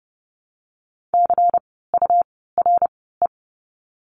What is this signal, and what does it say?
Morse code, heard as a keyed tone, about 30 wpm, spaced as for 17 wpm, 700 Hz. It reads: CURE
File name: Lesson 17f - Letter u - Words 30wpm